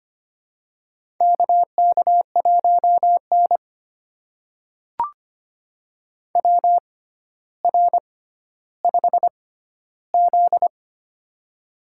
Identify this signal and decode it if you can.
Morse code, heard as a keyed tone, about 25 wpm, spaced as for 13 wpm, 700 Hz. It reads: KK1N WR5Z
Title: USA Call Signs - 4 Characters - 6x Character Spacing - 25wpm